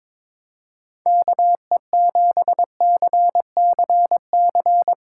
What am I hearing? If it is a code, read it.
Morse code, heard as a keyed tone, about 23 wpm, 700 Hz. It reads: KE7CCC
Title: Speed Racing - USA 2x3 Call Signs 22wpm